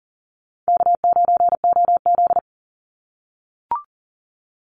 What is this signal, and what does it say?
Morse code, heard as a keyed tone, about 40 wpm, 700 Hz. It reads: K9OZ
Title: Easy Call Signs - 40wpm